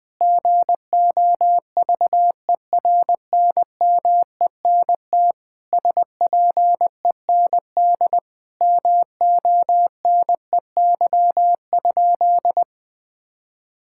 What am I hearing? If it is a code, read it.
Morse code, heard as a keyed tone, about 20 wpm, 700 Hz. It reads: GOVERNMENT SPEND MONEY?
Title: Sentences from Top 1000 Words - Review Part 5 - 20wpm